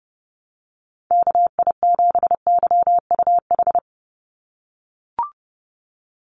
Morse code, heard as a keyed tone, about 32 wpm, 700 Hz. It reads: KI7YUH
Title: USA Call Signs 30wpm